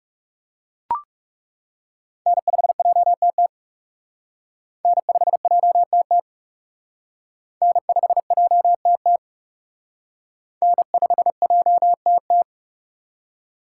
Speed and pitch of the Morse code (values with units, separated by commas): 38 wpm, 700 Hz